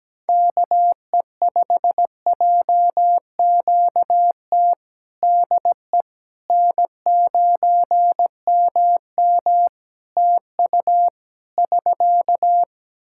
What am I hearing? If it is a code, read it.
Morse code, heard as a keyed tone, about 17 wpm, 700 Hz. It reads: KE5JQT DE N9MM TU <SK>
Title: K1USN SST QSO Practice 17wpm